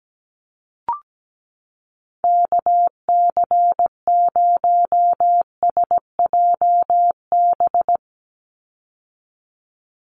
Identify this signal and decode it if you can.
Morse code, heard as a keyed tone, about 17 wpm, 700 Hz. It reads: KC0SJB
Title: USA Call Signs 17wpm